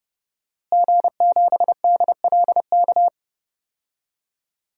Morse code, heard as a keyed tone, about 30 wpm, 700 Hz. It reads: G7DLK
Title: DXCC-100 Call Signs - 5 Characters - Repeated 2x - 30wpm